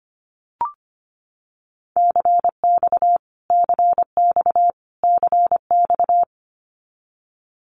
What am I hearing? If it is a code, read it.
Morse code, heard as a keyed tone, about 25 wpm, 700 Hz. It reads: CX CX CX
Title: Abbreviations Repeated 3x - 25wpm